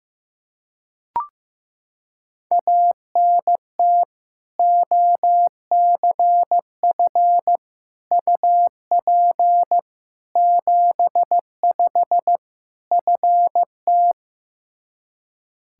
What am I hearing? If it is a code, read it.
Morse code, heard as a keyed tone, about 15 wpm, 700 Hz. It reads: ANT OCF UP 75 FT
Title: Practice QSO - Exchange 2 - Info - 15wpm